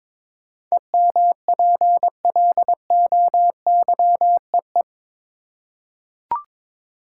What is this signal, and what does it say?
Morse code, heard as a keyed tone, about 22 wpm, 700 Hz. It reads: EMPLOYEE
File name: practice / 22wpm / Eight Letter Words - 22wpm